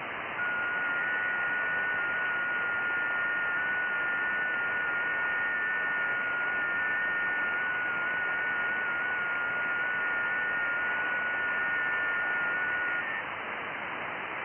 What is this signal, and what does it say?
FT8.
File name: IC-7100